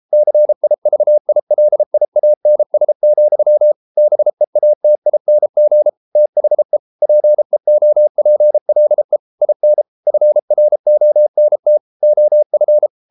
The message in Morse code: CIVILIANS, BEATING THE PEOPLE IN FRONT OF

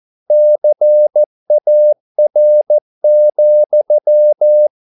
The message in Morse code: CAR,

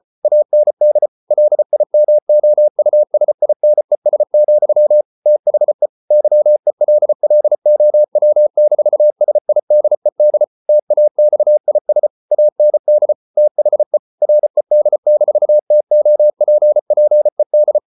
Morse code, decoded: AND LIMOUSINES, THE YELLOW-SIDED TAXIS AND THE RED-TOPPED

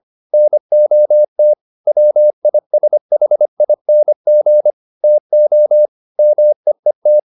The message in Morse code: NOT WISHING TO MEET

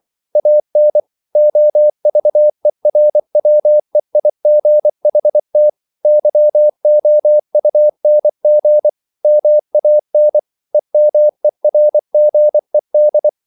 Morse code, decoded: AN OVERWEIGHT YOUNG MAN EMERGED

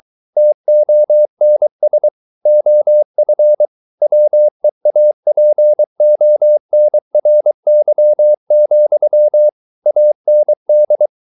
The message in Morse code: TONS OF WEAPONRY, AND

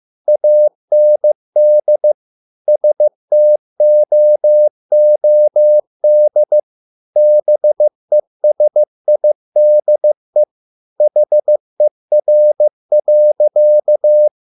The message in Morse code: AND STOOD BESIDE HER.

